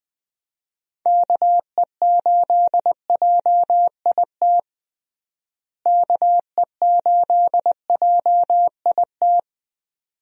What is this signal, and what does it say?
Morse code, heard as a keyed tone, about 20 wpm, 700 Hz. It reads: KE8JIT KE8JIT
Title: USA Call Signs - 6 Characters - Repeated 2x - 20wpm